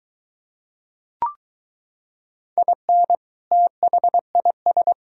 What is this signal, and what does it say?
Morse code, heard as a keyed tone, about 23 wpm, 700 Hz. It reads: IN THIS